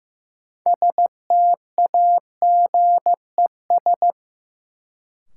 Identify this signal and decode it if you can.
Morse code, heard as a keyed tone, about 15 wpm, 700 Hz. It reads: STAGES
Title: Sets of 5 Words Encore - New Word Drill - 15wpm